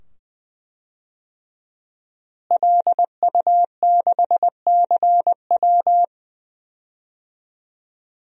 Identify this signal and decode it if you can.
Morse code, heard as a keyed tone, about 20 wpm, 700 Hz. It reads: LU6CW
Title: DXCC-100 Call Signs - 5 Characters - Repeated 2x - 20wpm